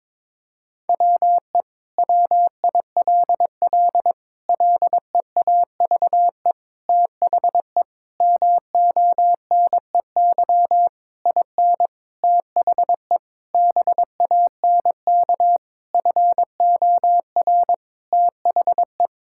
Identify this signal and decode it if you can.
Morse code, heard as a keyed tone, about 22 wpm, 700 Hz. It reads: WE WILL LEAVE THE MONEY IN THE BANK FOR THE
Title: Sentences from Top 1000 Words - Review Part 6 - 22wpm